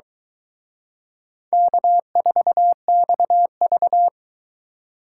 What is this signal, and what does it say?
Morse code, heard as a keyed tone, about 22 wpm, 700 Hz. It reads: K4XV